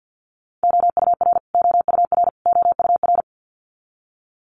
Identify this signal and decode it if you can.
Morse code, heard as a keyed tone, about 50 wpm, 700 Hz. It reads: OUR OUR OUR
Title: Top 300 Words Repeated 50wpm